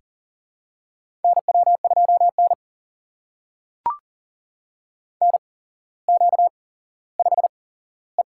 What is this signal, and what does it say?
Morse code, heard as a keyed tone, about 39 wpm, spaced as for 17 wpm, 700 Hz. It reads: NW2N NQ5E
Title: USA Call Signs - 4 Characters - 8x Character Spacing - 40wpm